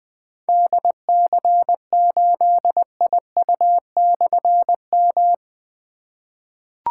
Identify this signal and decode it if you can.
Morse code, heard as a keyed tone, about 20 wpm, 700 Hz. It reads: DC8IU/M
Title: Difficult Call Signs 20wpm